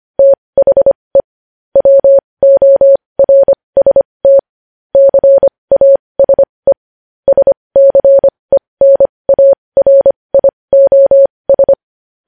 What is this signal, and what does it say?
Morse code, heard as a keyed tone, about 25 wpm, spaced as for 20 wpm, 550 Hz. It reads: THE WORST CASE SCENARIOS
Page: News headlines in Morse code at 20 WPM